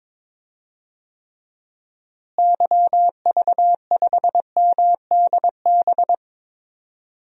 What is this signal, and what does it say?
Morse code, heard as a keyed tone, about 22 wpm, 700 Hz. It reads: YV5MDB
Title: DXCC-100 Call Signs - 6 Characters - Repeated 2x - 22wpm